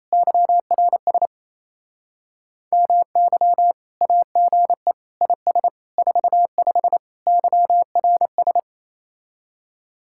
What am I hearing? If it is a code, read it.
Morse code, heard as a keyed tone, about 36 wpm, 700 Hz. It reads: YRS MY AGE IS 45 YRS